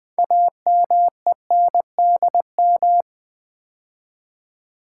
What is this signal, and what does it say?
Morse code, heard as a keyed tone, about 20 wpm, 700 Hz. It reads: AMENDM
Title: Build A Word - 3 Letters At A Time - 9-Letter Words - 20wpm